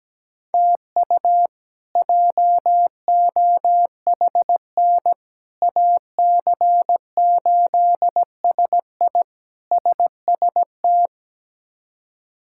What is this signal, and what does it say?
Morse code, heard as a keyed tone, about 17 wpm, 700 Hz. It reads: TU JOHN AC8SI SST
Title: K1USN SST QSO Practice - 17wpm